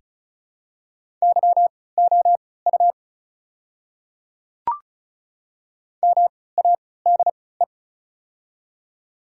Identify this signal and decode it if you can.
Morse code, heard as a keyed tone, about 36 wpm, spaced as for 20 wpm, 700 Hz. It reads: YOU MADE
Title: 3x Character Spacing - Top 200 Words - 35wpm